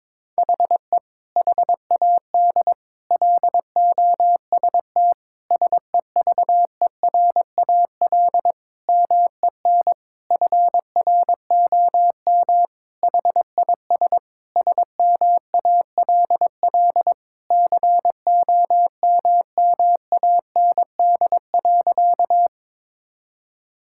Morse code, heard as a keyed tone, about 22 wpm, 700 Hz. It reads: HE HAD LOST SEVERAL MEN FROM HIS SMALL COMMAND.
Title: Sentences from Top 1600 Words 22wpm